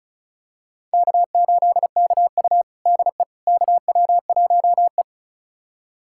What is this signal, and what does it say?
Morse code, heard as a keyed tone, about 36 wpm, 700 Hz. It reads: K8KU DE KW1E
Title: USA Callsign DE Callsign - 4 Characters - 35wpm